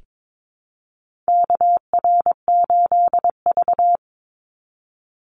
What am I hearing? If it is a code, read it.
Morse code, heard as a keyed tone, about 22 wpm, 700 Hz. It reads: KR8V
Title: Call Signs - 22wpm